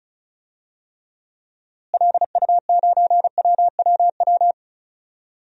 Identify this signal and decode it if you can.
Morse code, heard as a keyed tone, about 36 wpm, 700 Hz. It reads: LU9WWW
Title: DXCC-100 Call Signs - 6 Characters - Repeated 2x - 35wpm